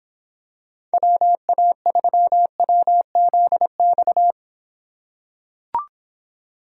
Morse code, heard as a keyed tone, about 26 wpm, 700 Hz. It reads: WA3WZX